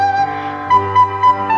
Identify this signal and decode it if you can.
There is no code.